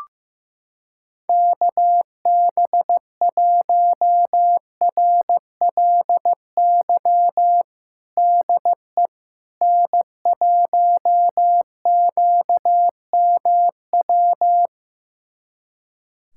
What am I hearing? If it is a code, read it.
Morse code, heard as a keyed tone, about 16 wpm, 700 Hz. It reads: KB1RLY DE N1QMW
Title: USA Callsign DE Callsign 15wpm